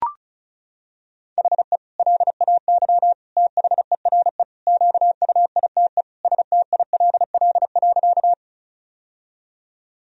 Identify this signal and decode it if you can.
Morse code, heard as a keyed tone, about 36 wpm, 700 Hz. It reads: HE LAY THERE QUITE STILL.
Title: Sentences from Top 1000 Words - Review Part 2 - 35wpm